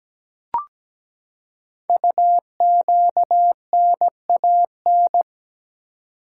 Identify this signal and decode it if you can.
Morse code, heard as a keyed tone, about 17 wpm, 700 Hz. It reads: UQNAN
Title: Code Groups - 5 Letters - 17wpm